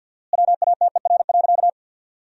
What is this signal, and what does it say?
Morse code, heard as a keyed tone, about 54 wpm, 700 Hz. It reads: WATER.
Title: 4x Word-Spacing - Sentences from Top 300 Words - 50wpm